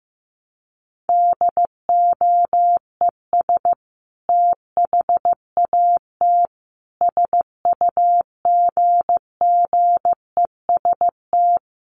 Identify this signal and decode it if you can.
Morse code, heard as a keyed tone, about 15 wpm, 700 Hz. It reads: DOES THAT SUGGEST
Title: Sentences from Top 1300 Words 15wpm